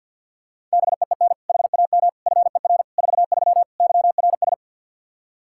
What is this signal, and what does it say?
Morse code, heard as a keyed tone, about 46 wpm, 700 Hz. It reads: BEEN HAM FER 43 YRS
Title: Practice QSO - Exchange 3 - More Info - 50wpm